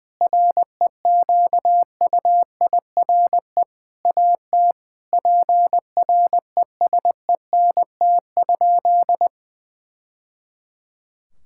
Morse code, heard as a keyed tone, about 21 wpm, 700 Hz. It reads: REQUIRE AT PRESENT?
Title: Sentences from Top 1000 Words - Review Part 6 - 20wpm